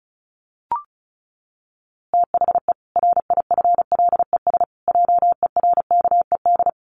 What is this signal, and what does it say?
Morse code, heard as a keyed tone, about 34 wpm, 700 Hz. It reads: THE RIFLES JERKED
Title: A Sound Of Thunder 35wpm